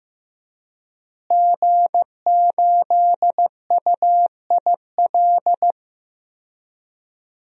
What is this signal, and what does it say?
Morse code, heard as a keyed tone, about 15 wpm, 700 Hz. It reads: G8UIL
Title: DXCC-100 Call Signs - 5 Characters - Repeated 3x - 15wpm